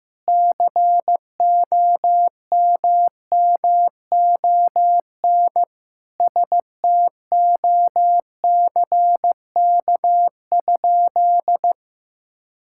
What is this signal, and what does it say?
Morse code, heard as a keyed tone, about 15 wpm, 700 Hz. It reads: COMMON STOCK?
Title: Sentences from Top 1000 Words - Review Part 5 - 15wpm